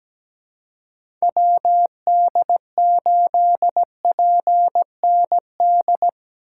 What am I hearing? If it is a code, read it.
Morse code, heard as a keyed tone, about 17 wpm, 700 Hz. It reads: WD8PND
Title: USA Call Signs - 6 Characters - 3x Character Spacing - 17wpm